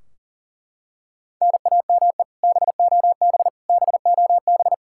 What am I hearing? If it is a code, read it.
Morse code, heard as a keyed tone, about 40 wpm, 700 Hz. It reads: NAME BOB BOB
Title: Practice QSO - Exchange 1 - Beginning - 40wpm